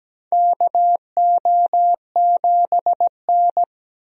Morse code, heard as a keyed tone, about 17 wpm, 700 Hz. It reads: KO7N
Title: USA Call Signs - 4 Characters - Repeated 3x - 17wpm